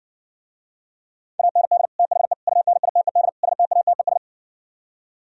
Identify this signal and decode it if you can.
Morse code, heard as a keyed tone, about 68 wpm, 700 Hz. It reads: AND THE UNITED STATES